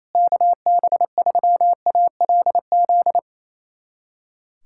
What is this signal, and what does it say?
Morse code, heard as a keyed tone, about 28 wpm, 700 Hz. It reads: KB3ALZ
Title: USA Callsign DE Callsign 28wpm